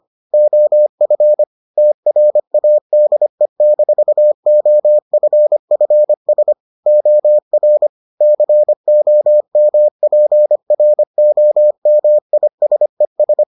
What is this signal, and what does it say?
Morse code, heard as a keyed tone, about 25 wpm, 600 Hz. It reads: OF TRADE-OFFS OR COMPROMISES